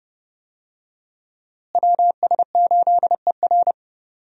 Morse code, heard as a keyed tone, about 29 wpm, 700 Hz. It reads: WS8ER
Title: USA Call Signs - 5 Characters - Repeated 3x - 30wpm